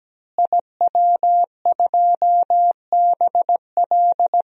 Morse code, heard as a keyed tone, about 17 wpm, 700 Hz. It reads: IW2BL
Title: DXCC-100 Call Signs - 5 Characters - Repeated 3x - 17wpm